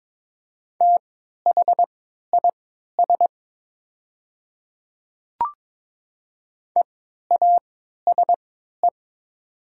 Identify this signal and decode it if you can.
Morse code, heard as a keyed tone, about 22 wpm, spaced as for 12 wpm, 700 Hz. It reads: THIS EASE
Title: 3x Character Spacing - Top 300 Words - 22wpm